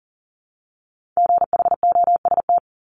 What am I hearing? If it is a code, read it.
Morse code, heard as a keyed tone, about 37 wpm, 700 Hz. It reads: GHOST
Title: Multi-letter Phonemes Repeated 3x 40wpm